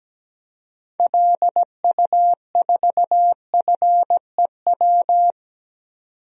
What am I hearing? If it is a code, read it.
Morse code, heard as a keyed tone, about 17 wpm, 700 Hz. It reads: LU4FEW
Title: DXCC-100 Call Signs - 6 Characters - 17wpm